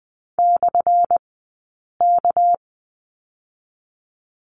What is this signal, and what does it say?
Morse code, heard as a keyed tone, about 20 wpm, 700 Hz. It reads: / K
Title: 197 Lesson 32d - Letter k - Two Characters 20wpm